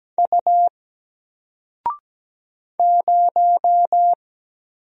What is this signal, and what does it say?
Morse code, heard as a keyed tone, about 17 wpm, 700 Hz. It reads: U 0
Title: Single Letter-Number Alt 17wpm